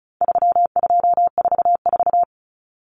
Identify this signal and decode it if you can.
Morse code, heard as a keyed tone, about 34 wpm, 700 Hz. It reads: 3244
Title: Common Cut Numbers - 4-Digit 35wpm